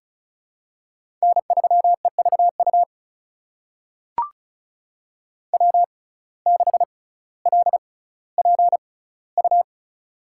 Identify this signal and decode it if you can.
Morse code, heard as a keyed tone, about 36 wpm, spaced as for 18 wpm, 700 Hz. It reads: N3EVU W6LPU
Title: USA Call Signs - 5 Characters - 6x Character Spacing - 35wpm